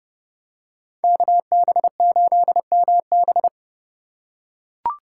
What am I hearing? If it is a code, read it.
Morse code, heard as a keyed tone, about 29 wpm, 700 Hz. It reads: KB8MB